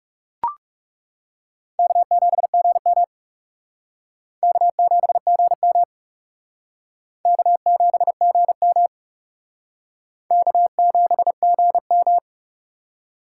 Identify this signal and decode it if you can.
Morse code, heard as a keyed tone, about 37 wpm, 700 Hz. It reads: K7GM K7GM K7GM K7GM
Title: Speed Racing - USA Call Signs 30wpm